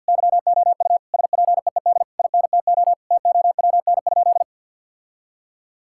Morse code, heard as a keyed tone, about 50 wpm, 700 Hz. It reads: YOU SPEED INTO TOWN?
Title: Sentences from Top 1000 Words - Review Part 5 - 50wpm